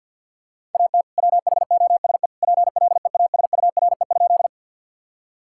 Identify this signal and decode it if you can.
Morse code, heard as a keyed tone, about 50 wpm, 700 Hz. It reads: AT WHOSE PLEASURE?
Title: Sentences from Top 1600 Words 50wpm